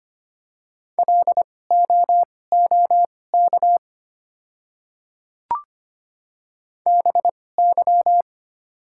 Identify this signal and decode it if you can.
Morse code, heard as a keyed tone, about 25 wpm, spaced as for 18 wpm, 700 Hz. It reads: LOOK BY
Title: 2x Character Spacing - Top 100 Words - 25wpm